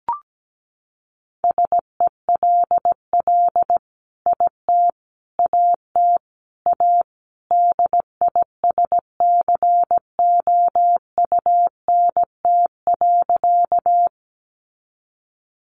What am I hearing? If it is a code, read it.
Morse code, heard as a keyed tone, about 17 wpm, 700 Hz. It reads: SELL IT AT A DISCOUNT.
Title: Sentences from Top 1200 Words 17wpm